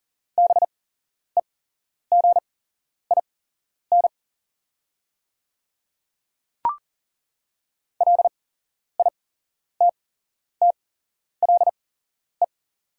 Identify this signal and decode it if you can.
Morse code, heard as a keyed tone, about 40 wpm, spaced as for 11 wpm, 700 Hz. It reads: BEGIN LITTLE
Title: 8x Character Spacing - Top 300 Words - 40wpm